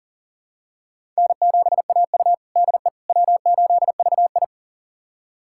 Morse code, heard as a keyed tone, about 40 wpm, 700 Hz. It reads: N7AU DE W8VI
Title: USA Callsign DE Callsign - 4 Characters - 40wpm